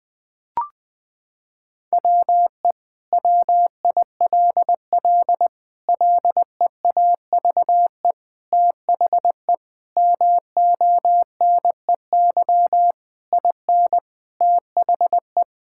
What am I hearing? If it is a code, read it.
Morse code, heard as a keyed tone, about 20 wpm, 700 Hz. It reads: WE WILL LEAVE THE MONEY IN THE
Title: Sentences from Top 1000 Words - Review Part 6 - 20wpm